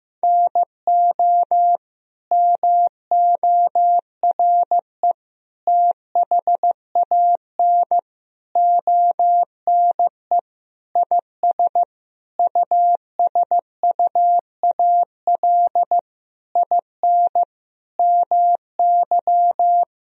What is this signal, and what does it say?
Morse code, heard as a keyed tone, about 15 wpm, 700 Hz. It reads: NO MORE THAN ONE IS USUAL IN MY
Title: Sentences from Top 1000 Words - Review Part 2 - 15wpm